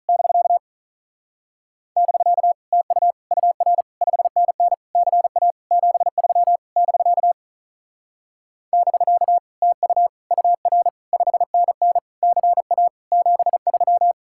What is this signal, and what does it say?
Morse code, heard as a keyed tone, about 46 wpm, 700 Hz. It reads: <BK> <BK> TU UR 5NN CA 73 <BK> <BK> TU UR 5NN CA 73